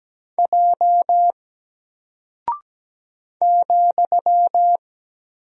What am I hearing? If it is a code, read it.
Morse code, heard as a keyed tone, about 17 wpm, 700 Hz. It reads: J ,
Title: Single Character (Letter:Number:Punctuation:Prosign) Alt 17wpm